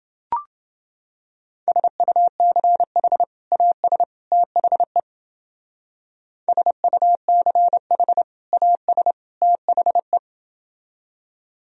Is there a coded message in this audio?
Yes: Morse code from 1.7 to 10.2 s, heard as a keyed tone, about 29 wpm, 700 Hz. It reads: SUCH AS THE SUCH AS THE